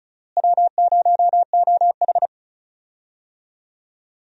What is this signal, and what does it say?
Morse code, heard as a keyed tone, about 34 wpm, 700 Hz. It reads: W0OH
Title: USA Call Signs - 4 Characters - Repeated 3x - 35wpm